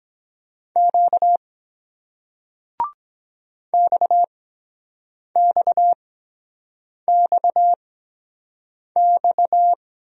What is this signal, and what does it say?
Morse code, heard as a keyed tone, about 23 wpm, spaced as for 7 wpm, 700 Hz. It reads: Q XXXX